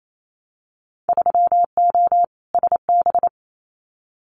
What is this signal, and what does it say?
Morse code, heard as a keyed tone, about 28 wpm, 700 Hz. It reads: 3O SB
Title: Winter Field Day Exchanges Encore 28wpm